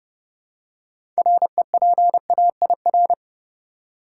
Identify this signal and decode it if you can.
Morse code, heard as a keyed tone, about 32 wpm, 700 Hz. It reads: REPAIR
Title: Six Letter Words - 30wpm